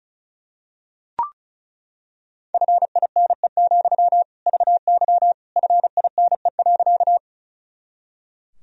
Morse code, heard as a keyed tone, about 36 wpm, 700 Hz. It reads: FINE, VY FINE.
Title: Sentences from Top 500 Words and Abbreviations 35wpm